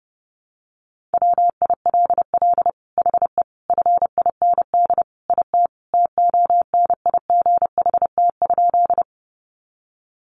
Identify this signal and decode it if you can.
Morse code, heard as a keyed tone, about 28 wpm, 700 Hz. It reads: WILL HE FIND IT TONIGHT?
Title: Sentences from Top 600 Words 30wpm